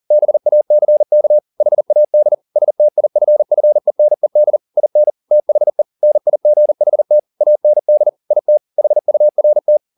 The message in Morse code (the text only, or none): BACK HAD STIFFENED IN THE NIGHT AND IT HURT